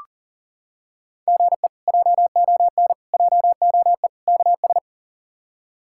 Morse code, heard as a keyed tone, about 40 wpm, 700 Hz. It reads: GE JON JOE KS
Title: K1USN SST QSO Practice - 40wpm